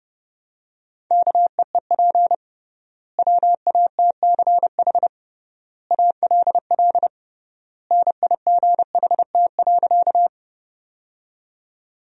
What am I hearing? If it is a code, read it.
Morse code, heard as a keyed tone, about 30 wpm, 700 Hz. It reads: KEEP WATCH ALL NIGHT.
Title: 3x Word-Spacing - Sentences from Top 300 Words - 30wpm